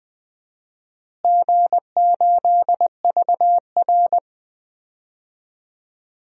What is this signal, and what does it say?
Morse code, heard as a keyed tone, about 20 wpm, 700 Hz. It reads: G8VR
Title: DXCC-100 Call Signs - 4 Characters - Repeated 2x - 20wpm